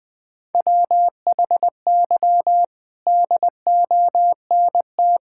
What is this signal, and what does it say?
Morse code, heard as a keyed tone, about 20 wpm, 700 Hz. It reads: WHY DONT
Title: Sentences with 4-Word Phrases - Spoken Beginning - 20wpm